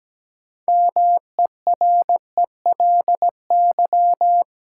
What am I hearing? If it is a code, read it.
Morse code, heard as a keyed tone, about 17 wpm, 700 Hz. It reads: MERELY
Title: Sentences from Top 1400 Words - New Word Drill - 17wpm